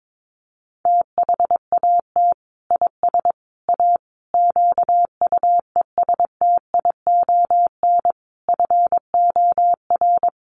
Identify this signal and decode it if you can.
Morse code, heard as a keyed tone, about 22 wpm, 700 Hz. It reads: THAT IS A QUESTION FOR
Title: Sentences from Top 600 Words 22wpm